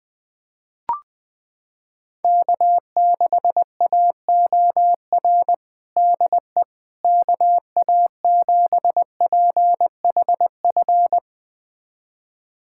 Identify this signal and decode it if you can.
Morse code, heard as a keyed tone, about 20 wpm, 700 Hz. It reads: K6AOR DE KA7PHF
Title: USA Callsign DE Callsign 20wpm